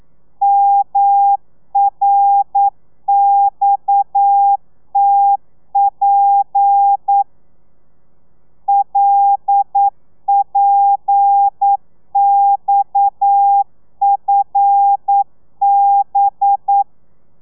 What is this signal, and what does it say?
Morse code, heard as a keyed tone, about 9 wpm, 800 Hz. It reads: MRXTP LPXFB
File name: telegrafia / i2tfj / Lez7 40